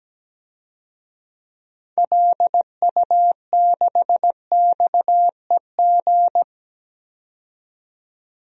Morse code, heard as a keyed tone, about 17 wpm, 700 Hz. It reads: LU6XEG
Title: DXCC-100 Call Signs - 6 Characters - 17wpm